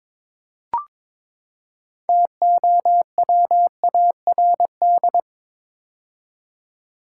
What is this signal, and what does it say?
Morse code, heard as a keyed tone, about 22 wpm, 700 Hz. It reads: TOWARD
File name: practice / 22wpm / Build A Phrase - 5-Word Phrases - 22wpm